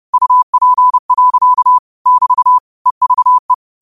Morse code, heard as a keyed tone, about 30 wpm, 1000 Hz. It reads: UP. <BT> EVE